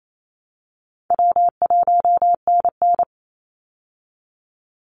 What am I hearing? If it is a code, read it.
Morse code, heard as a keyed tone, about 28 wpm, 700 Hz. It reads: W1NN
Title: Easy Call Signs Repeated 3x - 28wpm